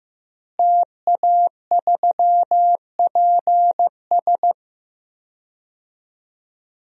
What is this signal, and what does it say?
Morse code, heard as a keyed tone, about 15 wpm, 700 Hz. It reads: TA3PS
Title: DXCC-100 Call Signs - 5 Characters - Repeated 3x - 15wpm